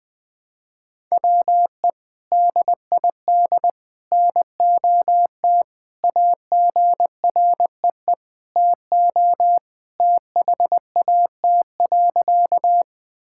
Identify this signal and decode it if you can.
Morse code, heard as a keyed tone, about 20 wpm, 700 Hz. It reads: WE DID NOT AGREE TO THAT.
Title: Sentences with 3-Word Phrases - Spoken Beginning - 20wpm